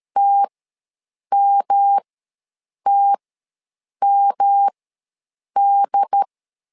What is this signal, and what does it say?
Morse code, heard as a keyed tone, about 13 wpm, spaced as for 7 wpm, 800 Hz. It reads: TMTMD
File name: LEZIONI / 3 MWRO / 3 M ws 12wpm 1745